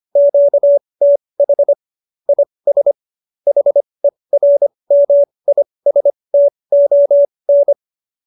Morse code, heard as a keyed tone, about 25 wpm, spaced as for 20 wpm, 570 Hz. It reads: QTH IS HERMISTON